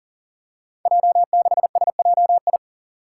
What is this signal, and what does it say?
Morse code, heard as a keyed tone, about 40 wpm, 700 Hz. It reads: J6SJI
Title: Five Challenging Letter-Number Combinations 40wpm